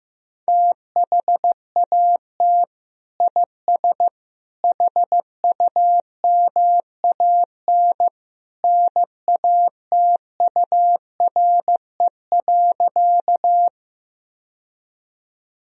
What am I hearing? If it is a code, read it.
Morse code, heard as a keyed tone, about 15 wpm, 700 Hz. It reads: THAT IS HUMAN NATURE.
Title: Sentences from Top 1000 Words - Review Part 3 - 15wpm